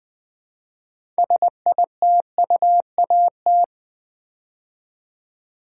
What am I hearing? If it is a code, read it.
Morse code, heard as a keyed tone, about 20 wpm, 700 Hz. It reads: SITUAT
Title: Build A Word - 3 Letters At A Time - 8-Letter Words - 20wpm